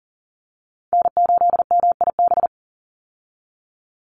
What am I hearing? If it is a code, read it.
Morse code, heard as a keyed tone, about 40 wpm, 700 Hz. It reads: N8MIB
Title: USA Call Signs 40wpm